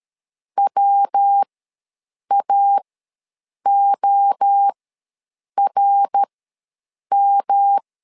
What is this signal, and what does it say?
Morse code, heard as a keyed tone, about 13 wpm, spaced as for 7 wpm, 800 Hz. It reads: WAORM
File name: LEZIONI / 3 MWRO / 3 MWRO ws 12wpm 1914